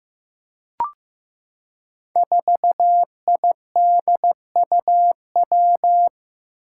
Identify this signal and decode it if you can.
Morse code, heard as a keyed tone, about 15 wpm, 700 Hz. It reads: 4IDUW
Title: Five Challenging Letter-Number Combinations 15wpm